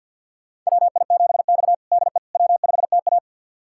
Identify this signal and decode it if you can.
Morse code, heard as a keyed tone, about 48 wpm, 700 Hz. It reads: WI7X DE W5TA
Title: USA Callsign DE Callsign - 4 Characters - 50wpm